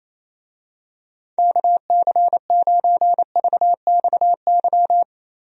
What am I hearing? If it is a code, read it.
Morse code, heard as a keyed tone, about 28 wpm, 700 Hz. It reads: KC9VXY
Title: USA Call Signs - 6 Characters - Repeated 2x - 28wpm